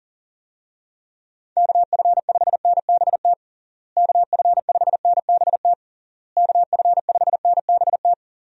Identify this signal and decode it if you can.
Morse code, heard as a keyed tone, about 37 wpm, 700 Hz. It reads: KF5NBT KF5NBT KF5NBT
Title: USA Call Signs - 6 Characters - Repeated 3x - 40wpm